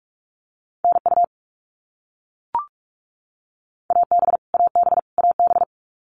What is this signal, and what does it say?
Morse code, heard as a keyed tone, about 44 wpm, 700 Hz. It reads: NU AB AB AB
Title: Canadian Provinces and Territories - Repeated 3x - 45wpm